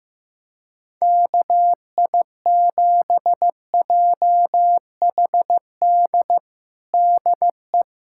Morse code, heard as a keyed tone, about 15 wpm, 700 Hz. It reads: KI7JHD DE
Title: USA Callsign DE Callsign 15wpm